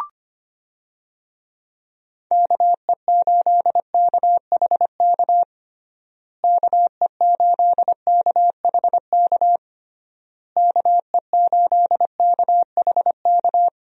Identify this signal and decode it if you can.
Morse code, heard as a keyed tone, about 25 wpm, 700 Hz. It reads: KE8KHK KE8KHK KE8KHK
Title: USA Call Signs - 6 Characters - Repeated 3x - 25wpm